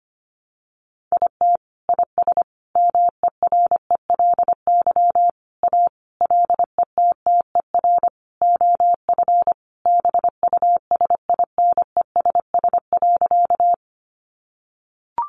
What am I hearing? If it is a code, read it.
Morse code, heard as a keyed tone, about 25 wpm, 700 Hz. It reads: IT IS MERELY A LETTER OF BUSINESS.
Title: Sentences from Top 1400 Words 25wpm